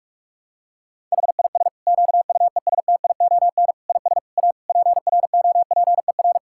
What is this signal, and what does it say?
Morse code, heard as a keyed tone, about 46 wpm, 700 Hz. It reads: HIS QUESTION IS A PROPER
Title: Sentences from Top 1000 Words - Review Part 5 - 45wpm